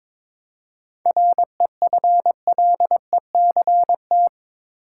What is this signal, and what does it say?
Morse code, heard as a keyed tone, about 22 wpm, 700 Hz. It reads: REFLECT
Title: Seven Letter Words - 22wpm